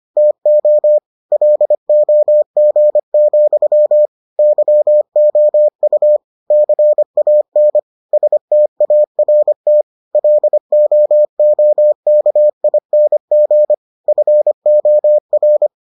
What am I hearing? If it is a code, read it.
Morse code, heard as a keyed tone, about 25 wpm, 600 Hz. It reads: TO LOG, YOU CAN START LOOKING FOR